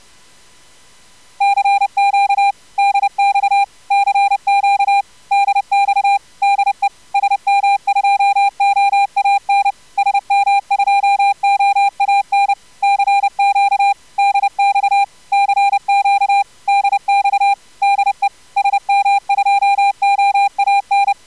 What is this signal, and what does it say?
Morse code, heard as a keyed tone, about 30 wpm, 800 Hz. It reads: CQ DX CQ DX DE SM2OAN SM2OAN CQ DX CQ DX DE SM2OAN
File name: CW jitter with corr-100-10-50